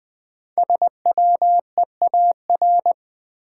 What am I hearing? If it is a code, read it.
Morse code, heard as a keyed tone, about 20 wpm, 700 Hz. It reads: SWEAR
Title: 5 Letter Words - First 3 Letters Given - 20wpm